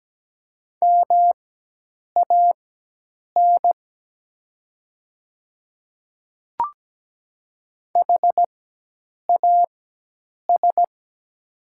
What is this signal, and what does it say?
Morse code, heard as a keyed tone, about 17 wpm, spaced as for 8 wpm, 690 Hz. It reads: MAN HAS
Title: 4x Character Spacing - Top 500 Words - 17wpm